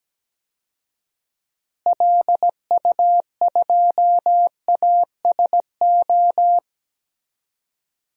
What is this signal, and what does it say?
Morse code, heard as a keyed tone, about 17 wpm, 700 Hz. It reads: LU2ASO
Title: DXCC-100 Call Signs - 6 Characters - Repeated 3x - 17wpm